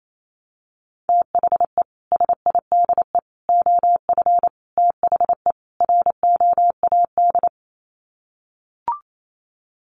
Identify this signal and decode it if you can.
Morse code, heard as a keyed tone, about 28 wpm, 700 Hz. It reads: THE SIDE OF THE ROAD
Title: Sets of 5 Words 28wpm